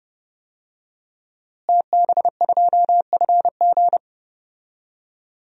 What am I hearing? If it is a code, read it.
Morse code, heard as a keyed tone, about 30 wpm, 700 Hz. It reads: TB2FG
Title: DXCC-100 Call Signs - 5 Characters - Repeated 3x - 30wpm